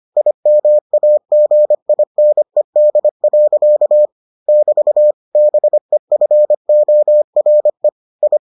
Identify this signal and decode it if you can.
Morse code, heard as a keyed tone, about 25 wpm, 600 Hz. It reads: IMAGINED. <BT> BEFORE I